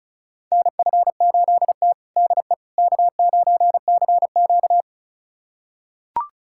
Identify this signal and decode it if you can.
Morse code, heard as a keyed tone, about 34 wpm, 700 Hz. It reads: NF8T DE K9CQ
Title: USA Callsign DE Callsign - 4 Characters - 35wpm